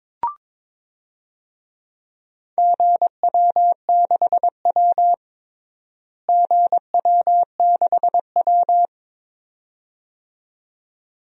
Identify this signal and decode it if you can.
Morse code, heard as a keyed tone, about 22 wpm, 700 Hz. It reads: GW6W GW6W
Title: DXCC-100 Call Signs - 4 Characters - Repeated 2x - 22wpm